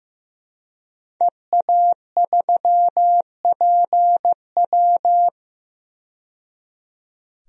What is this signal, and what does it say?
Morse code, heard as a keyed tone, about 15 wpm, 700 Hz. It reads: EA3PW
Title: DXCC-100 Call Signs - 5 Characters - 15wpm